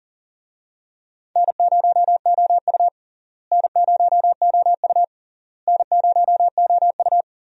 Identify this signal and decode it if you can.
Morse code, heard as a keyed tone, about 40 wpm, 700 Hz. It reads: N0OU N0OU N0OU
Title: USA Call Signs - 4 Characters - Repeated 3x - 40wpm